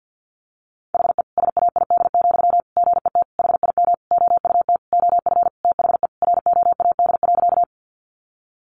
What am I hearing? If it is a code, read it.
Morse code, heard as a keyed tone, about 50 wpm, 700 Hz. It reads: HE SAID, GET HIM OUT OF THE ROAD.
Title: Sentences from Top 400 Words 50wpm